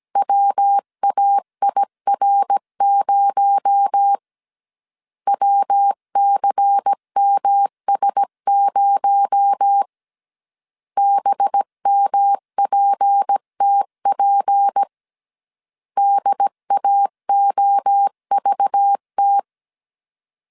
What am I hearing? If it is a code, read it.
Morse code, heard as a keyed tone, about 17 wpm, 800 Hz. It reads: WAIR0 WCMS0 BMPTP DAOVT
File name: CW / DET / 1 6 1500C S C 16wpm 2733